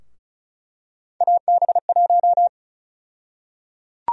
Morse code, heard as a keyed tone, about 36 wpm, 700 Hz. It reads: AB1
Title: USA Call Sign Prefixes - 35wpm